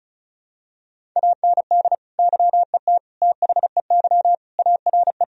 Morse code, heard as a keyed tone, about 36 wpm, 700 Hz. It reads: AND YET THEY ARE
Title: Sentences from Top 1000 Words - Review Part 4 - 35wpm